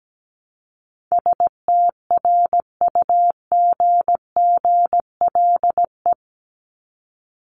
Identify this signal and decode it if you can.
Morse code, heard as a keyed tone, about 17 wpm, 700 Hz. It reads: STRUGGLE
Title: Top 1751-2000 Words 17wpm